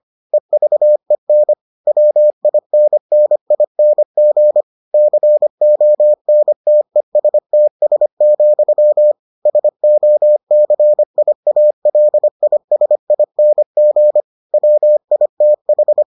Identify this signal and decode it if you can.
Morse code, heard as a keyed tone, about 25 wpm, 600 Hz. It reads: EVEN WINNING CONTESTS, SOCIALISING WITH